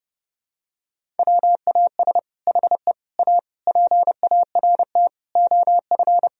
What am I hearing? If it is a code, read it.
Morse code, heard as a keyed tone, about 30 wpm, 700 Hz. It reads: WAS HE A PART OF
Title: Sentences from Top 1000 Words - Review Part 4 - 30wpm